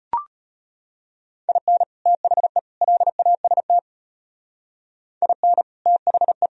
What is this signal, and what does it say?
Morse code, heard as a keyed tone, about 31 wpm, 700 Hz. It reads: IN THE LAST IN THE